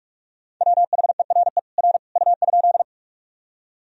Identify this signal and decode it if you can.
Morse code, heard as a keyed tone, about 44 wpm, 700 Hz. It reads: WHERE R U?
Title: Sentences from Top 200 Words and Abbreviations 45wpm